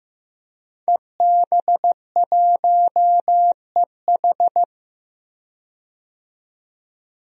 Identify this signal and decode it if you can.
Morse code, heard as a keyed tone, about 16 wpm, 700 Hz. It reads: EB1EH
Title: DXCC-100 Call Signs - 5 Characters - 15wpm